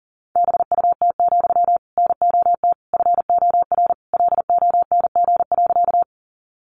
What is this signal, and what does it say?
Morse code, heard as a keyed tone, about 40 wpm, 700 Hz. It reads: BUT, NOT FOR LONG.
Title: Sentences from Top 600 Words 40wpm